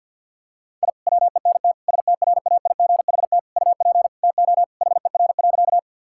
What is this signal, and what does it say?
Morse code, heard as a keyed tone, about 52 wpm, 700 Hz. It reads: I WENT STRAIGHT UP TO HER.
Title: Sentences from Top 1000 Words - Review Part 6 - 50wpm